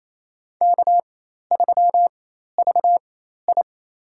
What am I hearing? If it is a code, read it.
Morse code, heard as a keyed tone, about 27 wpm, spaced as for 13 wpm, 700 Hz. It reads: K3VI
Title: USA Call Signs - 4x Character Spacing - 28wpm